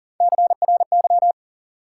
Morse code, heard as a keyed tone, about 37 wpm, 700 Hz. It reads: CRY
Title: Sets of 5 Words Encore - New Word Drill - 40wpm